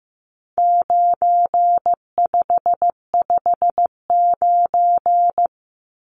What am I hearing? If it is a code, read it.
Morse code, heard as a keyed tone, about 15 wpm, 700 Hz. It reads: 9559
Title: Numbers - 4-Digit 15wpm